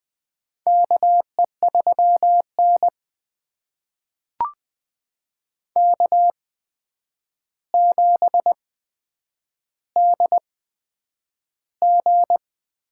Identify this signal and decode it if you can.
Morse code, heard as a keyed tone, about 20 wpm, spaced as for 9 wpm, 700 Hz. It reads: KE3N K7DG
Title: USA Call Signs - 4 Characters - 8x Character Spacing - 20wpm